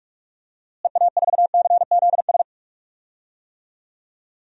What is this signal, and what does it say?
Morse code, heard as a keyed tone, about 46 wpm, 700 Hz. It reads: EA4CZS
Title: DXCC-100 Call Signs - 6 Characters - Repeated 3x - 45wpm